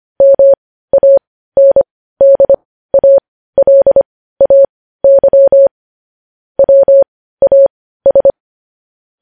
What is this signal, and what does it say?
Morse code, heard as a keyed tone, about 25 wpm, spaced as for 15 wpm, 550 Hz. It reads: MANDALAY WAS
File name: podcast / News headlines-1x-25x15